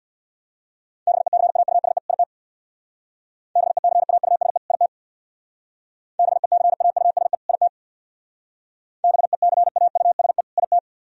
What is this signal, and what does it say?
Morse code, heard as a keyed tone, about 61 wpm, 700 Hz. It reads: BECAUSE IT BECAUSE IT BECAUSE IT BECAUSE IT
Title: Speed-Racing - Sets of 2 Words 50wpm